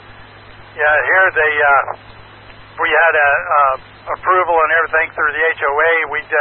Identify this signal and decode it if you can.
Emergency Alert System (SAME) tones (header bursts), heard as harsh buzzing bursts, with speech. It is heard from 0.7 to 5.3 s.